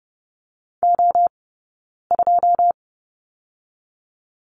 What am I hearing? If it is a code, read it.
Morse code, heard as a keyed tone, about 30 wpm, 700 Hz. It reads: O 2
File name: 092 Lesson 16d - Letter c - Two Characters 30wpm